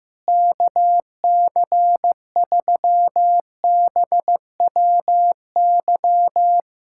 Morse code, heard as a keyed tone, about 15 wpm, 700 Hz. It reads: KC3BWY
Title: USA Callsign DE Callsign - 6 Characters - 15wpm